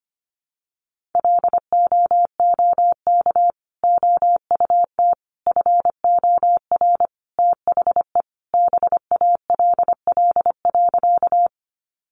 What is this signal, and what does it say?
Morse code, heard as a keyed tone, about 25 wpm, 700 Hz. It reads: LOOK OUT FOR THE BALL.
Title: Sentences from Top 900 Words 25wpm